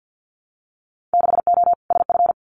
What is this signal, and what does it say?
Morse code, heard as a keyed tone, about 50 wpm, 700 Hz. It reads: 6O SF
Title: Winter Field Day Exchanges Encore 50wpm